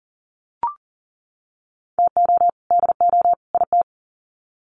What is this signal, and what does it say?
Morse code, heard as a keyed tone, about 40 wpm, 700 Hz. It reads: TO DO IT
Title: Sets of 3 Words 40wpm